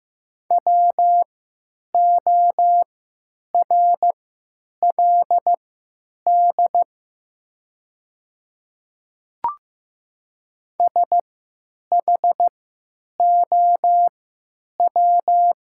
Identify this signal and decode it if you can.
Morse code, heard as a keyed tone, about 15 wpm, spaced as for 8 wpm, 700 Hz. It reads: WORLD SHOW
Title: 3x Character Spacing - Top 200 Words - 15wpm